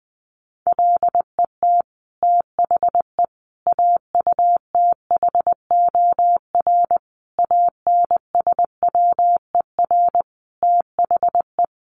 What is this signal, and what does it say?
Morse code, heard as a keyed tone, about 20 wpm, 700 Hz. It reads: LET THE AUTHOR ANSWER THE
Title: Sentences from Top 1200 Words 20wpm